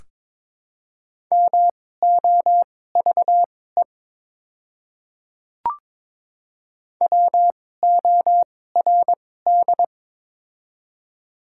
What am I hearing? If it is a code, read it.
Morse code, heard as a keyed tone, about 22 wpm, spaced as for 16 wpm, 700 Hz. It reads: MOVE WORD
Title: 2x Character Spacing - Top 300 Words - 22wpm